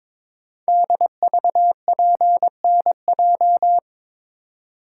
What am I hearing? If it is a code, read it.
Morse code, heard as a keyed tone, about 22 wpm, 700 Hz. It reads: DVPNJ
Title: Code Groups - 5 Letters - 22wpm